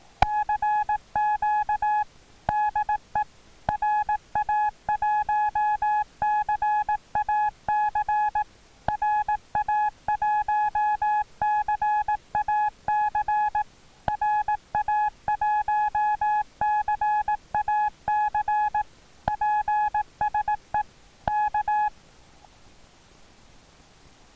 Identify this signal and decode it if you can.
Morse code, heard as a keyed tone, about 18 wpm, 840 Hz. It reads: CQ DE RA1CAC RA1CAC RA1CAC PSE K
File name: attachments / 1cq noagc 2cq agcin 3cq agcout